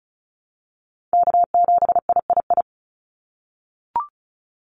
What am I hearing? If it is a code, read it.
Morse code, heard as a keyed tone, about 34 wpm, 700 Hz. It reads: K7III
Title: NA SOTA Chaser Callsigns 35wpm